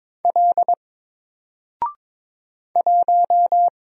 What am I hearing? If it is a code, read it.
Morse code, heard as a keyed tone, about 22 wpm, 700 Hz. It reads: L 1
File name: Single-Letter-Number-22wpm